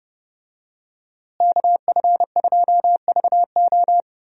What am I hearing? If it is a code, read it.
Morse code, heard as a keyed tone, about 30 wpm, 700 Hz. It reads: KF2VO
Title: USA Call Signs - 5 Characters - Repeated 3x - 30wpm